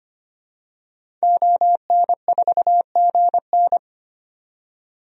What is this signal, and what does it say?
Morse code, heard as a keyed tone, about 25 wpm, 700 Hz. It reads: ON4GN
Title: DXCC-100 Call Signs - 5 Characters - 25wpm